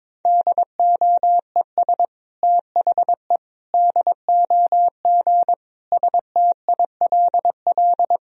Morse code, heard as a keyed tone, about 22 wpm, 700 Hz. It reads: DOES THE DOG STILL
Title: Sentences from Top 1000 Words - Review Part 3 - 22wpm